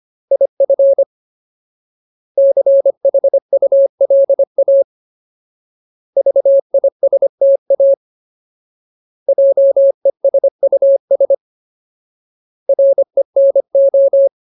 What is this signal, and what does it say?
Morse code, heard as a keyed tone, about 25 wpm, 550 Hz. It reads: IF CHULA VISTA JESUS RENO
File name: Ultimate Word List / 25WPM Ultimate